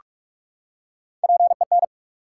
Morse code, heard as a keyed tone, about 44 wpm, 700 Hz. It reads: PEN
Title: Three Letter Words - 45wpm